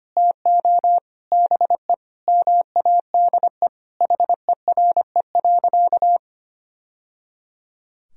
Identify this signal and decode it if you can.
Morse code, heard as a keyed tone, about 25 wpm, 700 Hz. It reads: TO BE MADE HERE.
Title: Sentences from Top 1000 Words - Review Part 4 - 25wpm